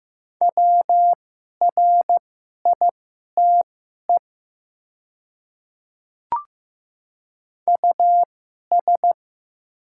Morse code, heard as a keyed tone, about 15 wpm, spaced as for 11 wpm, 700 Hz. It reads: WRITE US
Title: 2x Character Spacing - Top 200 Words - 15wpm